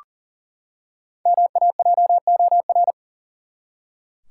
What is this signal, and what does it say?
Morse code, heard as a keyed tone, about 37 wpm, 700 Hz. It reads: MAJOR
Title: Five Letter Words - 40wpm